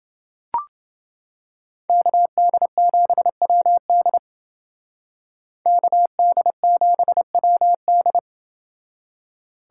Morse code, heard as a keyed tone, about 28 wpm, 700 Hz. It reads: KD7WD KD7WD